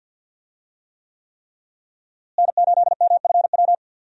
Morse code, heard as a keyed tone, about 47 wpm, 700 Hz. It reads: N8MFW
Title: USA Call Signs - 5 Characters - Repeated 3x - 50wpm